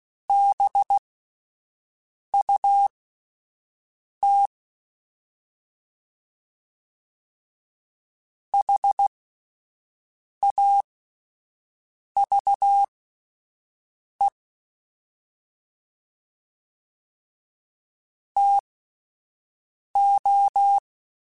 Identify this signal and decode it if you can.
Morse code, heard as a keyed tone, about 16 wpm, spaced as for 5 wpm, 800 Hz. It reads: BUT HAVE TO
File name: QSOs Spaced